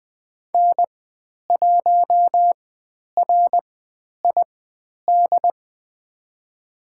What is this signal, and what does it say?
Morse code, heard as a keyed tone, about 20 wpm, spaced as for 10 wpm, 700 Hz. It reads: N1RID
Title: 109 Lesson 18g - Letter m - Call Signs 20wpm